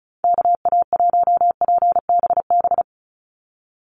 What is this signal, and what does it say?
Morse code, heard as a keyed tone, about 34 wpm, 700 Hz. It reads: KA1PBB
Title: USA Call Signs 35wpm